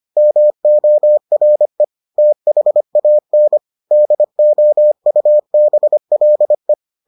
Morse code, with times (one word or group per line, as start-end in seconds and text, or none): 0.2-1.8 MORE
2.2-3.6 THAN
3.9-6.7 DOUBLE